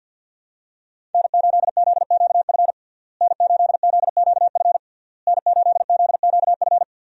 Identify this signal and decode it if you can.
Morse code, heard as a keyed tone, about 51 wpm, 700 Hz. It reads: N8ZQF N8ZQF N8ZQF
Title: USA Call Signs - 5 Characters - Repeated 3x - 50wpm